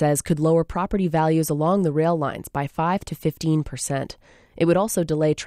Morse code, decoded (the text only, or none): none